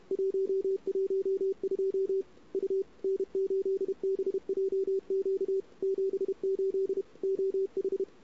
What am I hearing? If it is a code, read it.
Morse code, heard as a keyed tone, about 34 wpm, 410 Hz. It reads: 112 U N8BJQ 78 OH
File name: CW / 2008-ss-cw-k5tr-op-n5rz -23:00